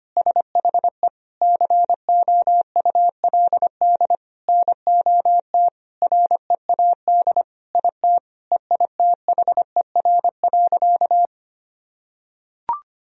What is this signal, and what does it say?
Morse code, heard as a keyed tone, about 25 wpm, 700 Hz. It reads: SHE COULD NOT READ IT EITHER.
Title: Sentences from Top 1000 Words - Review Part 4 - 25wpm